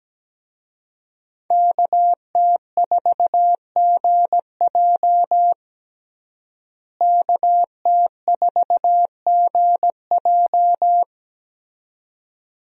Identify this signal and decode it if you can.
Morse code, heard as a keyed tone, about 17 wpm, 700 Hz. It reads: KT4GJ KT4GJ
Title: USA Call Signs - 5 Characters - Repeated 2x - 17wpm